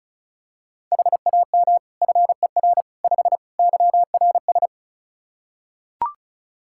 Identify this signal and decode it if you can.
Morse code, heard as a keyed tone, about 36 wpm, 700 Hz. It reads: HAM FER 5 YRS
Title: Practice QSO - Exchange 3 - More Info - 35wpm